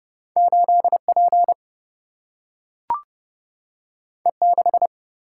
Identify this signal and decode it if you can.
Morse code, heard as a keyed tone, about 30 wpm, 700 Hz. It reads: 8P E6
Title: Single-Letter Single-Number 30wpm